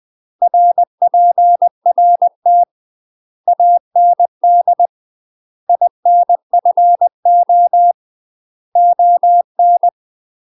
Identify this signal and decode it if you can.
Morse code, heard as a keyed tone, about 20 wpm, 700 Hz. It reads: RPRT AND INFO ON
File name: UK MASTER QSOs TRAINING / 20WPM / UK MASTER QSOs 0046